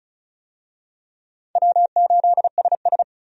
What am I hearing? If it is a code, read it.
Morse code, heard as a keyed tone, about 34 wpm, 700 Hz. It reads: W8SS
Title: USA Call Signs - 4 Characters - 35wpm